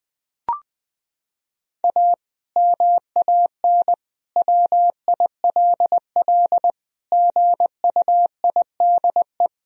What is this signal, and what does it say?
Morse code, heard as a keyed tone, about 20 wpm, 700 Hz. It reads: A MAN WILL GUIDE
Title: Sentences from Top 1000 Words - Review Part 3 - 20wpm